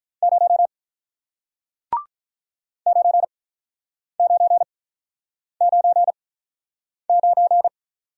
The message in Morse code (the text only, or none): 0 9999